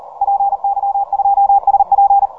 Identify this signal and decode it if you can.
Morse code, heard as a keyed tone, about 40 wpm, 800 Hz, with radio noise. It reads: PY2SP